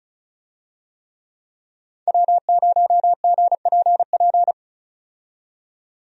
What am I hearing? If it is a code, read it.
Morse code, heard as a keyed tone, about 36 wpm, 700 Hz. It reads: W0GPP
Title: USA Call Signs - 5 Characters - 35wpm